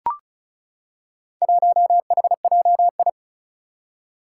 Morse code, heard as a keyed tone, about 36 wpm, 700 Hz. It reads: 1HJI
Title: Four Challenging Letter-Number Combinations 35wpm